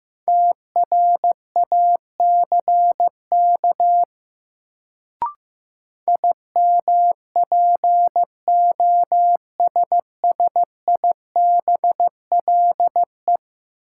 Morse code, heard as a keyed tone, about 15 wpm, 700 Hz. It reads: TRACK IMPOSSIBLE
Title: Sentences from Top 1100 Words - New Word Drill Alt - 15wpm